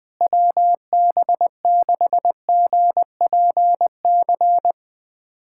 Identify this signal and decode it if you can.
Morse code, heard as a keyed tone, about 20 wpm, 700 Hz. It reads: WB6GPC
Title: USA Callsign DE Callsign - 6 Characters - 20wpm